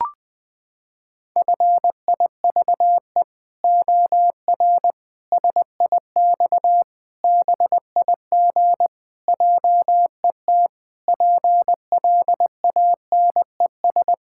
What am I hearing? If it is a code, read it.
Morse code, heard as a keyed tone, about 21 wpm, 700 Hz. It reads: FIVE OR SIX BIG JET PLANES
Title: Pangram Challenge v2 - 20wpm